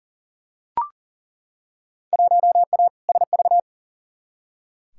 Morse code, heard as a keyed tone, about 40 wpm, 700 Hz. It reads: 1A SV
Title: Field Day Exchanges - 40wpm